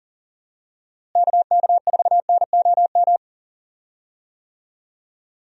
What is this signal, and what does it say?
Morse code, heard as a keyed tone, about 40 wpm, 700 Hz. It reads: KK4NOM
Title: USA Call Signs - 6 Characters - Repeated 3x - 40wpm